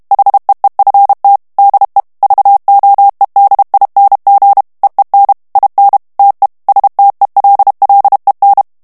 Morse code, heard as a keyed tone, about 32 wpm, 800 Hz. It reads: HEEFT DE VOEDING EEN IN TE STELLEN
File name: HOLLANDAIS texte clair 2 vitesse 30 mots minute